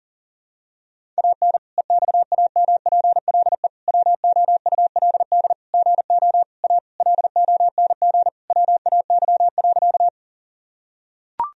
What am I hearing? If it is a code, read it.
Morse code, heard as a keyed tone, about 37 wpm, 700 Hz. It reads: AN EXAMPLE WOULD GO A LONG WAY.
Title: Sentences from Top 1000 Words - Review Part 1 - 40wpm